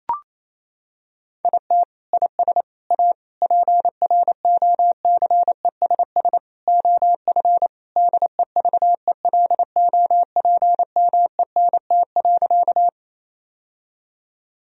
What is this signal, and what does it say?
Morse code, heard as a keyed tone, about 28 wpm, 700 Hz. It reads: IT IS A PROCESS OF DEVELOPMENT.
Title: Sentences from Top 1000 Words - Review Part 4 - 28wpm